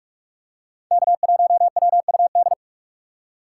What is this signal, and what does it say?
Morse code, heard as a keyed tone, about 44 wpm, 700 Hz. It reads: K1WUD
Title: USA Call Signs - 2x Character Spacing - 45wpm